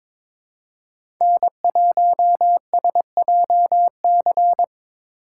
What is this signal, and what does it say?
Morse code, heard as a keyed tone, about 22 wpm, 700 Hz. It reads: N1SJC
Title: USA Call Signs - 5 Characters - 6x Character Spacing - 22wpm